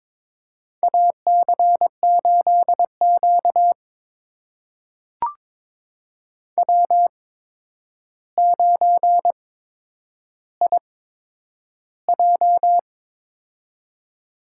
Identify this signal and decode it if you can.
Morse code, heard as a keyed tone, about 22 wpm, spaced as for 9 wpm, 700 Hz. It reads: AC8Q W9IJ
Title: USA Call Signs - 4 Characters - 8x Character Spacing - 22wpm